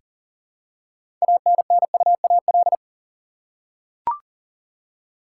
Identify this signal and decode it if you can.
Morse code, heard as a keyed tone, about 40 wpm, 700 Hz. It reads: ANNUAL
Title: Six Letter Words - 40wpm